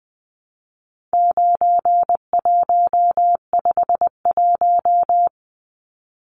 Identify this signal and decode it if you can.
Morse code, heard as a keyed tone, about 20 wpm, 700 Hz. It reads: 9151
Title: Numbers - 4-Digit 20wpm